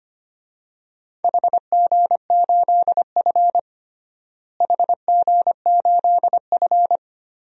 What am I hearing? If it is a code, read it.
Morse code, heard as a keyed tone, about 25 wpm, 700 Hz. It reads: HG8F HG8F
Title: DXCC-100 Call Signs - 4 Characters - Repeated 2x - 25wpm